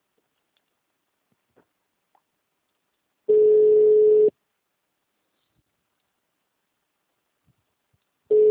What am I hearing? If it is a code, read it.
Ringback tone (Europe, 425 Hz).